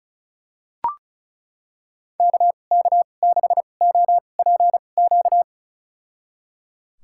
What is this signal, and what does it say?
Morse code, heard as a keyed tone, about 36 wpm, spaced as for 25 wpm, 700 Hz. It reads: KK6OPQ
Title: USA Call Signs - 6 Characters - 2x Character Spacing - 35wpm